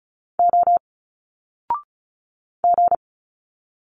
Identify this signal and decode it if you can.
Morse code, heard as a keyed tone, about 36 wpm, 700 Hz. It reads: O G
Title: Single Letters 35wpm